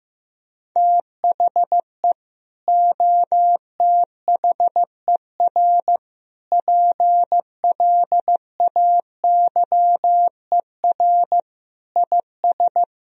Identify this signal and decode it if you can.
Morse code, heard as a keyed tone, about 15 wpm, 700 Hz. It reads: THE OTHER PLAYER IS
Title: Sentences from Top 1000 Words - Review Part 4 - 15wpm